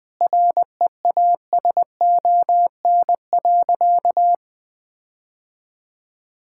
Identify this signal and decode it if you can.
Morse code, heard as a keyed tone, about 20 wpm, 700 Hz. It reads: REASON.
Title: Sentences with 8-Word Phrases - Spoken Beginning - 20wpm